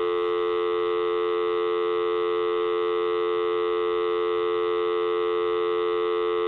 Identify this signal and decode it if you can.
Dial tone (North America), 350 + 440 Hz, with noise.
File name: dial-ringing-from-speaker-phone